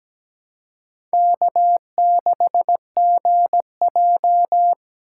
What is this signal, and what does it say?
Morse code, heard as a keyed tone, about 17 wpm, 700 Hz. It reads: K6GJ
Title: USA Call Signs - 2x Character Spacing - 17wpm